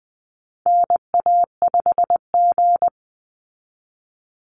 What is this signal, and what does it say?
Morse code, heard as a keyed tone, about 20 wpm, 700 Hz. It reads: NA5G
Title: Call Signs Repeated 3x - 20wpm